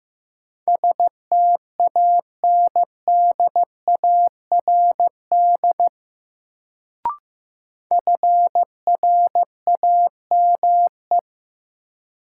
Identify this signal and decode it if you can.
Morse code, heard as a keyed tone, about 16 wpm, 700 Hz. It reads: STANDARD FRAME
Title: Sentences from Top 1000 Words - Review Part 6 - New Word Drill - 15wpm